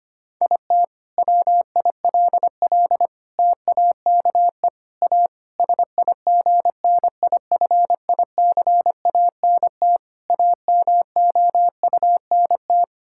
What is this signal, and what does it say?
Morse code, heard as a keyed tone, about 25 wpm, 700 Hz. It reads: IT WILL TAKE A SIGNIFICANT AMOUNT
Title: Sentences from Top 1000 Words - Review Part 6 - 25wpm